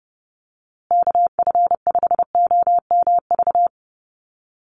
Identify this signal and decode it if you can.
Morse code, heard as a keyed tone, about 30 wpm, 700 Hz. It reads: KF5OMV
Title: USA Call Signs Repeated 2x 30wpm